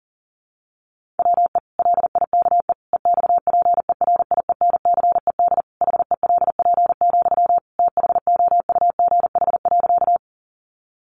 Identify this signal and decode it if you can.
Morse code, heard as a keyed tone, about 40 wpm, 700 Hz. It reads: WE LIKE EXPERIENCED HELP, THOUGH.
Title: Sentences from Top 1100 Words 40wpm